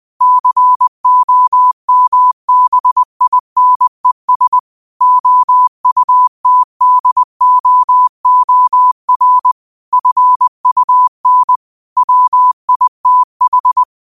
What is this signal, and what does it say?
Morse code, heard as a keyed tone, about 20 wpm, 1000 Hz. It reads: COMBINES OUTDOOR FUN WITH